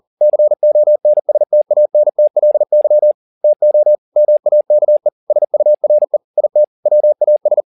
Morse code, heard as a keyed tone, about 40 wpm, 600 Hz. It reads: CONSTANTLY TO MAKE SURE IT WAS
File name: the old man and the sea 2 40wpm